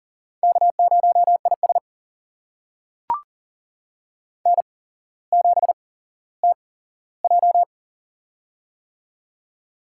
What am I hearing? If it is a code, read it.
Morse code, heard as a keyed tone, about 38 wpm, spaced as for 17 wpm, 700 Hz. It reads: K0IS N7TJ
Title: USA Call Signs - 4 Characters - 8x Character Spacing - 40wpm